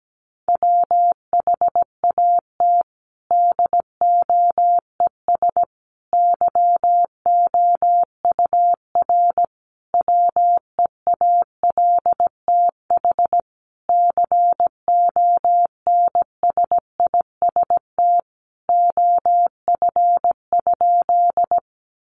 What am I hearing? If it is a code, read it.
Morse code, heard as a keyed tone, about 17 wpm, 700 Hz. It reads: WHAT DOES YOUR WEALTH CONSIST OF?
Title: Sentences from Top 1400 Words 17wpm